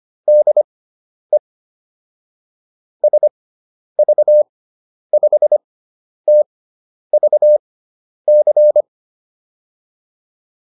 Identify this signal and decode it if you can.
Morse code, heard as a keyed tone, about 25 wpm, spaced as for 10 wpm, 600 Hz. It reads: DE SV5TVC